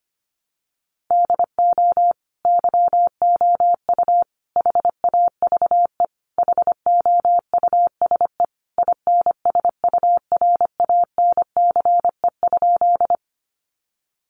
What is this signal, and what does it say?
Morse code, heard as a keyed tone, about 25 wpm, 700 Hz. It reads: DO YOU HAVE HOUSE INSURANCE?
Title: Sentences from Top 1100 Words 25wpm